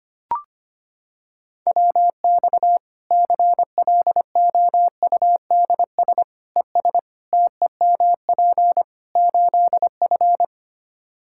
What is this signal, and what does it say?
Morse code, heard as a keyed tone, about 25 wpm, 700 Hz. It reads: WX CLOUDS ES TEMP 8F
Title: Practice QSO - Exchange 2 - Info - 25wpm